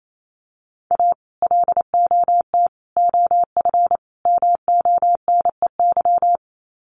Morse code, heard as a keyed tone, about 28 wpm, 700 Hz. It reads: A LOT OF MONEY
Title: Sets of 4 Words 28wpm